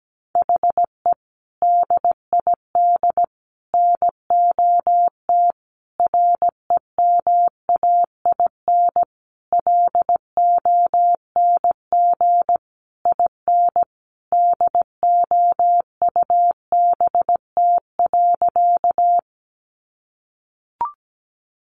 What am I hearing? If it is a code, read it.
Morse code, heard as a keyed tone, about 17 wpm, 700 Hz. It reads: HE DID NOT REMAIN LONG IN DOUBT.
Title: Sentences from Top 1500 Words 17wpm